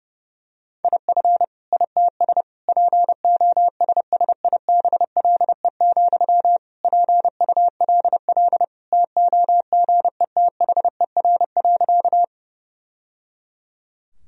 Morse code, heard as a keyed tone, about 30 wpm, 700 Hz. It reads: IF ITS POSSIBLE, PULL TOGETHER.
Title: Sentences from Top 1000 Words - Review Part 2 - 30wpm